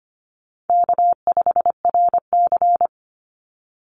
Morse code, heard as a keyed tone, about 25 wpm, 700 Hz. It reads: K5RC
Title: Easy Call Signs Repeated 2x - 25wpm